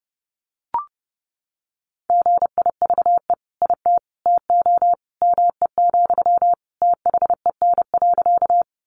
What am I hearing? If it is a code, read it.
Morse code, heard as a keyed tone, about 30 wpm, 700 Hz. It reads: GIVE IT TO ME, THEN.
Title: Sentences from Top 200 Words 30wpm